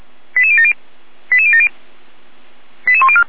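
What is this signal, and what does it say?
Five-tone selective call (ZVEI), heard as quick runs of five tones, tones of about 70 ms. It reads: 88070 88070 88104
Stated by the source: ZVEI five-tone call received on an SDR, channel 455.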